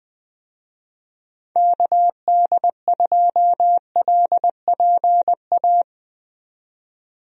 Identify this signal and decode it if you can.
Morse code, heard as a keyed tone, about 20 wpm, 700 Hz. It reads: KD2LPA
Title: USA Call Signs - 6 Characters - 20wpm